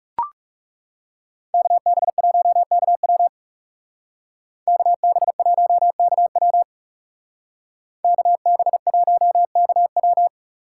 Morse code, heard as a keyed tone, about 39 wpm, 700 Hz. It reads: KB1KW KB1KW KB1KW